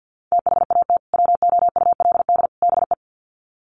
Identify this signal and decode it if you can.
Morse code, heard as a keyed tone, about 50 wpm, 700 Hz. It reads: THAT WOULD BE
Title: Sets of 3 Words 50wpm